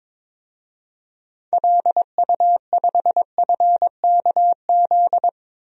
Morse code, heard as a keyed tone, about 22 wpm, 700 Hz. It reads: LU5FKZ
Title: DXCC-100 Call Signs - 6 Characters - 22wpm